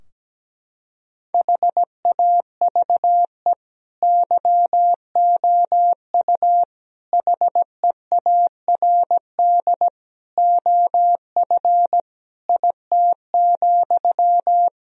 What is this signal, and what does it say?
Morse code, heard as a keyed tone, about 17 wpm, 700 Hz. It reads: HAVE YOU HEARD OF IT,
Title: Sentences from Top 1000 Words - Review Part 4 - 17wpm